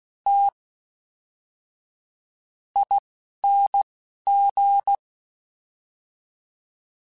Morse code, heard as a keyed tone, about 16 wpm, spaced as for 11 wpm, 800 Hz. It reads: T ING
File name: Special INGs / Special INGs-mp3s / Fast ing @10